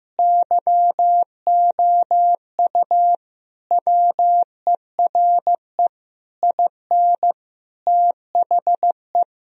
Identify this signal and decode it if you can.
Morse code, heard as a keyed tone, about 15 wpm, 700 Hz. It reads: YOU WERE IN THE
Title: Sentences from Top 1000 Words - Review Part 1 - 15wpm